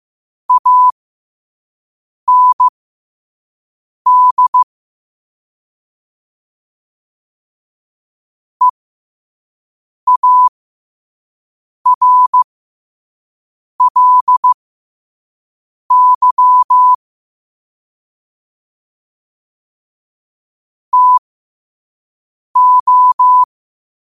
Morse code, heard as a keyed tone, about 15 wpm, spaced as for 5 wpm, 1000 Hz. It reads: AND EARLY TO
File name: qotdfiles / 5wpm